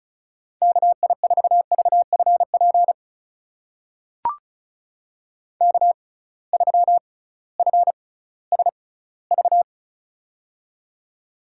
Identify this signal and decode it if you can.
Morse code, heard as a keyed tone, about 36 wpm, spaced as for 19 wpm, 700 Hz. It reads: KI4VFP K3FSV
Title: USA Call Signs - 6x Character Spacing - 35wpm